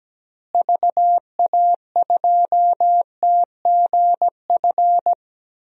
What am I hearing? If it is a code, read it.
Morse code, heard as a keyed tone, about 17 wpm, 700 Hz. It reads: VA2TGF
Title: DXCC-100 Call Signs - 6 Characters - Repeated 3x - 17wpm